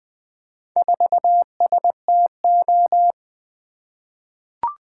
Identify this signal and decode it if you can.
Morse code, heard as a keyed tone, about 20 wpm, 700 Hz. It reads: 4STO
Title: Four Letter-Number Combinations - 20wpm